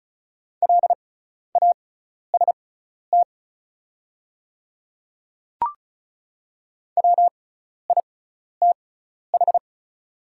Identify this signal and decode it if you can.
Morse code, heard as a keyed tone, about 34 wpm, spaced as for 12 wpm, 700 Hz. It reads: LAST WITH
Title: 6x Character Spacing - Top 300 Words - 35wpm